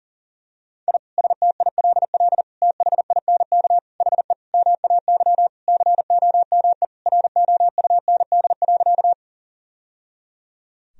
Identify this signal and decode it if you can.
Morse code, heard as a keyed tone, about 40 wpm, 700 Hz. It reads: I STILL THINK HE MAY COME ROUND.
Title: Sentences from Top 1000 Words - Review Part 1 - 40wpm